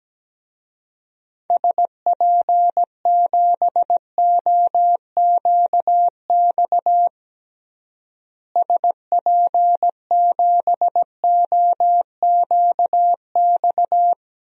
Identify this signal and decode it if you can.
Morse code, heard as a keyed tone, about 17 wpm, 700 Hz. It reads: SP7OQX SP7OQX
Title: DXCC-100 Call Signs - 6 Characters - Repeated 2x - 17wpm